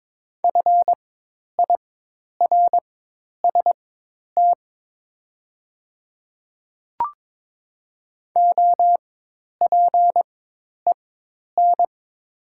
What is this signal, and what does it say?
Morse code, heard as a keyed tone, about 22 wpm, spaced as for 10 wpm, 700 Hz. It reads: FIRST OPEN
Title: 4x Character Spacing - Top 300 Words - 22wpm